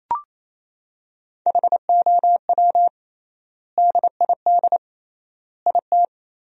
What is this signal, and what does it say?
Morse code, heard as a keyed tone, about 28 wpm, 700 Hz. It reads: HOW DID IT
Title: 3x Word-Spacing - Sentences from Top 300 Words - 28wpm